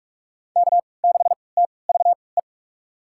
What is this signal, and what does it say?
Morse code, heard as a keyed tone, about 45 wpm, spaced as for 25 wpm, 700 Hz. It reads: K6TVE
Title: USA Call Signs - 5 Characters - 3x Character Spacing - 45wpm